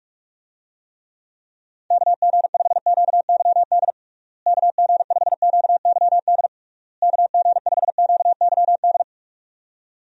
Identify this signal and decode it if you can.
Morse code, heard as a keyed tone, about 48 wpm, 700 Hz. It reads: KG5QYD KG5QYD KG5QYD
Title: USA Call Signs - 6 Characters - Repeated 3x - 45wpm